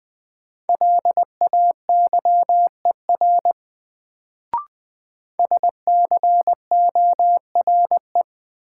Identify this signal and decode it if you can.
Morse code, heard as a keyed tone, about 20 wpm, 700 Hz. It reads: LAYER SCORE
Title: Sentences from Top 1600 Words - New Word Drill - 20wpm